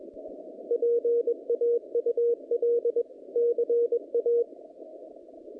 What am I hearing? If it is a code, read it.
Morse code, heard as a keyed tone, about 23 wpm, 470 Hz. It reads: PAUL CA
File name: contests / naqp-cw-k5tr-op-k5pi- -0000